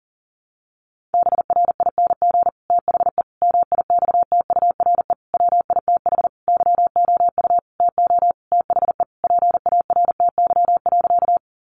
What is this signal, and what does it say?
Morse code, heard as a keyed tone, about 40 wpm, 700 Hz. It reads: BRING THE MIXTURE WITH YOU TO THE PARTY.
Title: Sentences from Top 1500 Words 40wpm